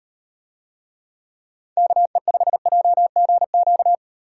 Morse code, heard as a keyed tone, about 36 wpm, 700 Hz. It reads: KE5JGQ